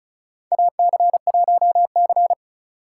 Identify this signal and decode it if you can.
Morse code, heard as a keyed tone, about 34 wpm, 700 Hz. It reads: AC1C
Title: USA Call Signs - 4 Characters - 3x Character Spacing - 35wpm